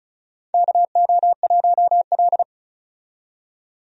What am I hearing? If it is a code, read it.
Morse code, heard as a keyed tone, about 34 wpm, 700 Hz. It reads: KO1L
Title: USA Call Signs - 4 Characters - 35wpm